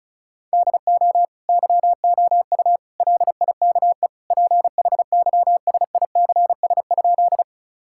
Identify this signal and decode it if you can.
Morse code, heard as a keyed tone, about 36 wpm, 700 Hz. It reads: DO YOU LIKE PHYSICS?
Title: Sentences from Top 1000 Words - Review Part 5 - 35wpm